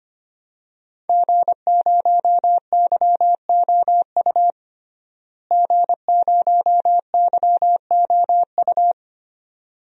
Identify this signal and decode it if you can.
Morse code, heard as a keyed tone, about 25 wpm, 700 Hz. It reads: G0YOU G0YOU
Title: DXCC-100 Call Signs - 5 Characters - Repeated 2x - 25wpm